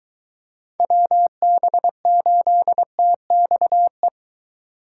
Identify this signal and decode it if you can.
Morse code, heard as a keyed tone, about 23 wpm, 700 Hz. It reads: WB8TXE